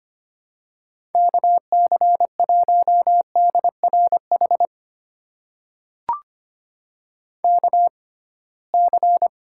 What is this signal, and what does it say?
Morse code, heard as a keyed tone, about 25 wpm, spaced as for 19 wpm, 700 Hz. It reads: KC1DRH KC
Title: USA Call Signs - 6 Characters - 6x Character Spacing - 25wpm